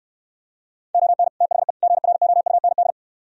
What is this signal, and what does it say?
Morse code, heard as a keyed tone, about 67 wpm, 700 Hz. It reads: ON THE GROUND